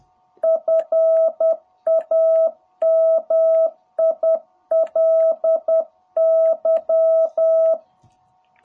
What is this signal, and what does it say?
Morse code, heard as a keyed tone, about 10 wpm, 660 Hz. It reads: FAMILY
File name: jh8cbh / 2021cw